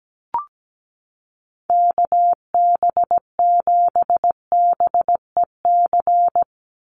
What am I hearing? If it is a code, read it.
Morse code, heard as a keyed tone, about 17 wpm, 700 Hz. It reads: KB7BEC
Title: USA Call Signs 17wpm